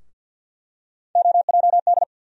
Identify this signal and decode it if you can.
Morse code, heard as a keyed tone, about 48 wpm, 700 Hz. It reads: OJD
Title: USA Call Sign Postfixes - 3 Letters 50wpm